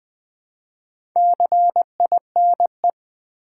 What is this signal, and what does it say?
Morse code, heard as a keyed tone, about 20 wpm, 700 Hz. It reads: CINE
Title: Build A Word - 2 Letters At A Time - 6-Letter Words - 20wpm